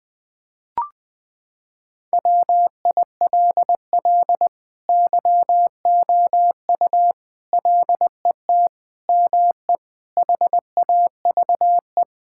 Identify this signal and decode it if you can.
Morse code, heard as a keyed tone, about 19 wpm, 700 Hz. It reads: WILL YOU LET ME HAVE
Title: Sentences from Top 1600 Words 20wpm